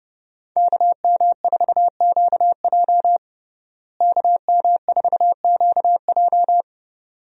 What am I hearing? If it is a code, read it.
Morse code, heard as a keyed tone, about 32 wpm, 700 Hz. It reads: KM4QJ KM4QJ
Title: USA Call Signs - 5 Characters - Repeated 2x - 30wpm